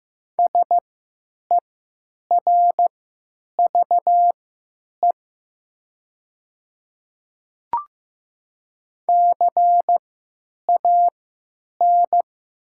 Morse code, heard as a keyed tone, about 15 wpm, spaced as for 9 wpm, 700 Hz. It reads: SERVE CAN
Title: 3x Character Spacing - Top 500 Words - 15wpm